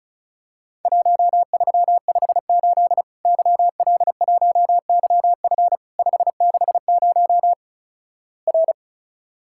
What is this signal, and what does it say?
Morse code, heard as a keyed tone, about 36 wpm, 700 Hz. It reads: 1358 YL1YF 560 R
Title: Worked All Europe DX Contest - QTC Exchange - 35wpm